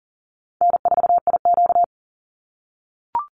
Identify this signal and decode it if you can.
Morse code, heard as a keyed tone, about 40 wpm, 700 Hz. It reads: N4IQ
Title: Call Signs - 40wpm